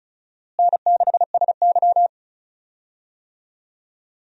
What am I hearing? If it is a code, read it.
Morse code, heard as a keyed tone, about 36 wpm, 700 Hz. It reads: N6SY
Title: USA Call Signs - 4 Characters - Repeated 3x - 35wpm